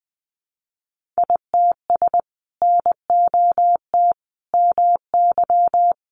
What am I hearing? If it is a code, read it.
Morse code, heard as a keyed tone, about 20 wpm, 700 Hz. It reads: ITS NOT MY
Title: Sentences from Top 600 Words 20wpm